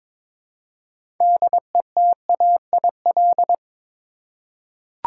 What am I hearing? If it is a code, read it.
Morse code, heard as a keyed tone, about 22 wpm, 700 Hz. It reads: DETAIL
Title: Six Letter Words - 22wpm